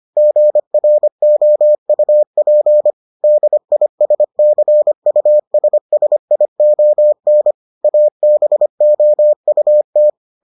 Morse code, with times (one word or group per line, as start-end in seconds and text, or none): 0.2-2.9 GROUP
3.2-7.5 DISCUSSION
7.8-10.1 ABOUT